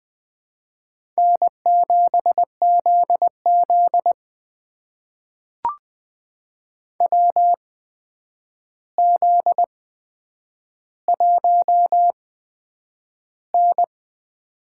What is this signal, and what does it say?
Morse code, heard as a keyed tone, about 20 wpm, spaced as for 9 wpm, 700 Hz. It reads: N7ZZ WZ1N
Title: USA Call Signs - 4 Characters - 8x Character Spacing - 20wpm